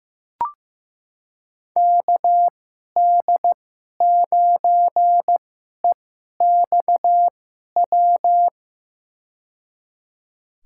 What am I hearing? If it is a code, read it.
Morse code, heard as a keyed tone, about 15 wpm, spaced as for 11 wpm, 700 Hz. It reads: KD9EXW
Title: USA Call Signs - 2x Character Spacing - 15wpm